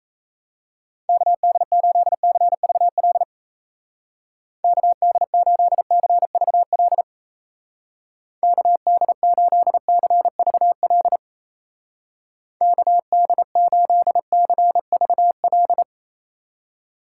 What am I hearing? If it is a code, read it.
Morse code, heard as a keyed tone, about 37 wpm, 700 Hz. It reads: KD8CVL KD8CVL KD8CVL KD8CVL